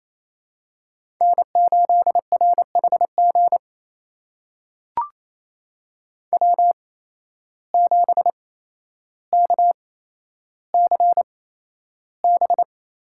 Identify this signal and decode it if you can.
Morse code, heard as a keyed tone, about 28 wpm, spaced as for 12 wpm, 700 Hz. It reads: N8RHG W7KCB
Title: USA Call Signs - 8x Character Spacing - 28wpm